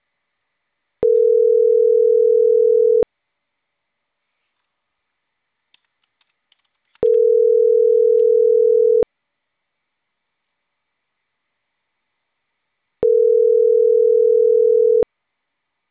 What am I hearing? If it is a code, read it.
Ringback tone (North America), 440 + 480 Hz.